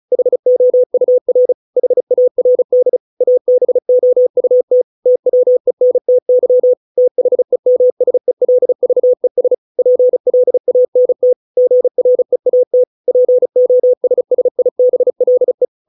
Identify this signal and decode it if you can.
Morse code, heard as a keyed tone, about 34 wpm, 500 Hz. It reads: HOUR HARD ABOUT TWENTY THEMSELVES PLANT GREAT POSSIBLE